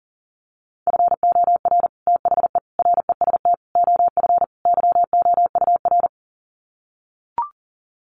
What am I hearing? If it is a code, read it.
Morse code, heard as a keyed tone, about 40 wpm, 700 Hz. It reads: FOR THE REST OF YOUR
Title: Sets of 5 Words Encore 40wpm